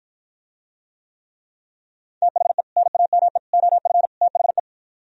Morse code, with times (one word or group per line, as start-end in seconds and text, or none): 2.2-2.6 THE
2.8-3.4 NAME
3.5-4.1 OF
4.2-4.6 THE